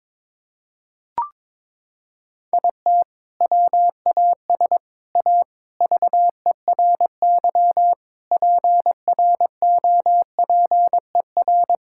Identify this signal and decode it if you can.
Morse code, heard as a keyed tone, about 22 wpm, 700 Hz. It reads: IT WAS A VERY PROPER
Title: Sentences from Top 1000 Words - Review Part 5 - 22wpm